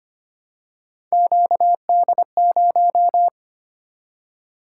Morse code, heard as a keyed tone, about 25 wpm, 700 Hz. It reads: QD0
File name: Three Letter-Number Combinations - 25wpm